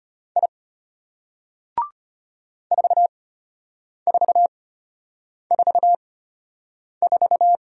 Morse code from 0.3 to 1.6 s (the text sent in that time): I